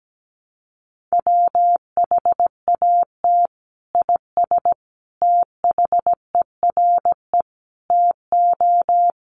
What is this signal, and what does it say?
Morse code, heard as a keyed tone, about 17 wpm, 700 Hz. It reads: WHAT IS THERE TO
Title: Sentences from Top 400 Words 17wpm